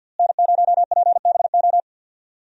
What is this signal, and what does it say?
Morse code, heard as a keyed tone, about 53 wpm, 700 Hz. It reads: N0PBO
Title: USA Call Signs - 5 Characters - Repeated 2x - 50wpm